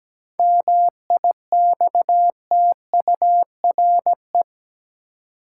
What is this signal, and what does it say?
Morse code, heard as a keyed tone, about 17 wpm, 700 Hz. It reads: MIXTURE
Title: Seven Letter Words - 17wpm